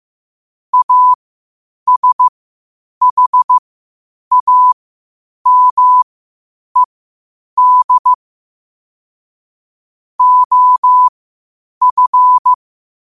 Morse code, heard as a keyed tone, about 15 wpm, spaced as for 8 wpm, 1000 Hz. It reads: ASHAMED OF
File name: qotdfiles / 8wpm